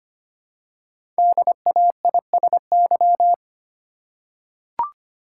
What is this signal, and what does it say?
Morse code, heard as a keyed tone, about 25 wpm, 700 Hz. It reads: DAISY
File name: practice / 25wpm / Easy Words with Hints - 25wpm